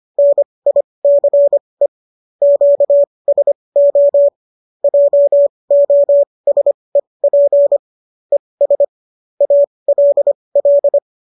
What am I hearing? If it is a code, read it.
Morse code, heard as a keyed tone, about 25 wpm, spaced as for 20 wpm, 570 Hz. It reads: NICE QSO JOSEP ES ALL